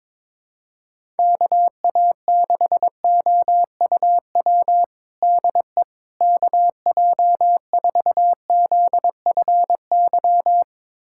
Morse code, heard as a keyed tone, about 22 wpm, 700 Hz. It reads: KA6OUW DE KJ4ZFY
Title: USA Callsign DE Callsign - 6 Characters - 22wpm